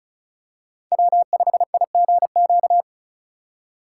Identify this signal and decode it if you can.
Morse code, heard as a keyed tone, about 36 wpm, 700 Hz. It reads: W5IGQ
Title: USA Call Signs - 5 Characters - 4x Character Spacing - 35wpm